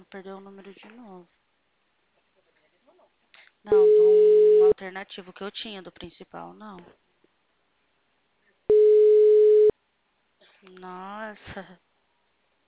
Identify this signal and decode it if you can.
Ringback tone (Europe, 425 Hz), with speech.
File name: out- -2004- - - .38